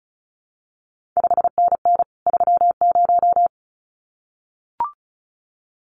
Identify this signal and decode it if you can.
Morse code, heard as a keyed tone, about 36 wpm, 700 Hz. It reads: 5NN 30
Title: CQ WW DX Contest Exchange Encore - 35wpm